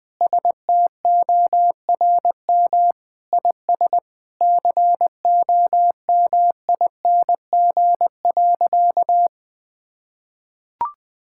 Morse code, heard as a keyed tone, about 20 wpm, 700 Hz. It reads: STORM IS COMING.
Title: Sentences with 2-Word Phrases - Spoken Beginning - 20wpm